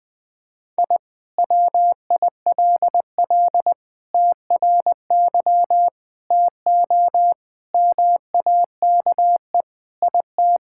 Morse code, heard as a keyed tone, about 20 wpm, 700 Hz. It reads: I WILL TRY TO MAKE IT
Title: Sentences with 3-Word Phrases - 20wpm